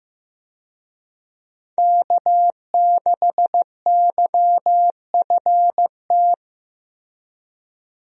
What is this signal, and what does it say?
Morse code, heard as a keyed tone, about 14 wpm, 700 Hz. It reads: K6YFT
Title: USA Call Signs - 5 Characters - Repeated 3x - 15wpm